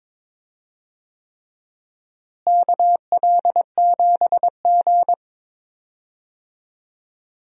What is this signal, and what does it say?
Morse code, heard as a keyed tone, about 22 wpm, 700 Hz. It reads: KL7G
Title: DXCC-100 Call Signs - 4 Characters - Repeated 3x - 22wpm